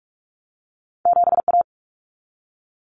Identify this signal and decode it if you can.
Morse code, heard as a keyed tone, about 46 wpm, 700 Hz. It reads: 7A
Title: Common Cut Numbers - 2-Digit 45wpm